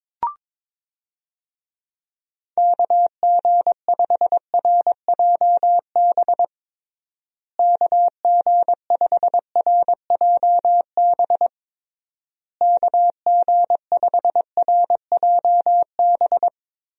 Morse code, heard as a keyed tone, about 22 wpm, 700 Hz. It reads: KG5RJB KG5RJB KG5RJB
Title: USA Call Signs - 6 Characters - Repeated 3x - 22wpm